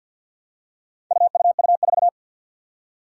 Morse code, heard as a keyed tone, about 47 wpm, 700 Hz. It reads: UUU4
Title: Four Challenging Letter-Number Combinations 50wpm